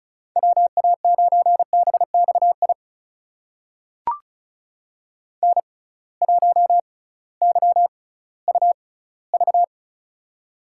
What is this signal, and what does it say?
Morse code, heard as a keyed tone, about 36 wpm, spaced as for 19 wpm, 700 Hz. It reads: WA9BXI N1YUV
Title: USA Call Signs - 6x Character Spacing - 35wpm